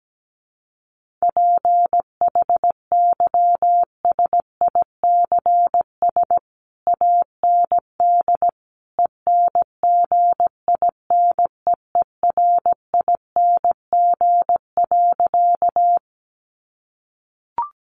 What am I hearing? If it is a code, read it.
Morse code, heard as a keyed tone, about 17 wpm, 700 Hz. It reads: PHYSICS AND ENGINEERING.
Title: Sentences from Top 1500 Words 17wpm